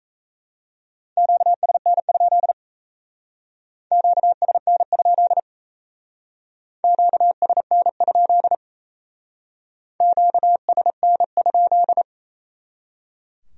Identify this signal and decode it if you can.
Morse code, heard as a keyed tone, about 34 wpm, 700 Hz. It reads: QSN? QSN? QSN? QSN?
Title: Speed-Racing Q-Codes - 28wpm